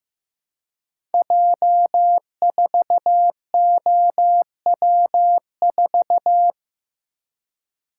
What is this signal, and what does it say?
Morse code, heard as a keyed tone, about 15 wpm, 700 Hz. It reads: J4OW4
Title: Code Groups - 5 Letter:Numbers - 15wpm